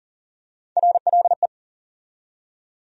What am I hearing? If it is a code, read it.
Morse code, heard as a keyed tone, about 40 wpm, 700 Hz. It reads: RLE
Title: USA Call Sign Postfixes - 3 Letters 40wpm